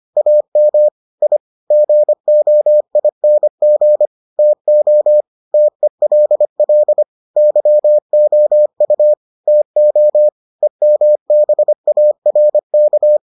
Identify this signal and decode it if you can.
Morse code, heard as a keyed tone, about 25 wpm, 600 Hz. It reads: AM I GOING TO TELL YOU TO EMBARK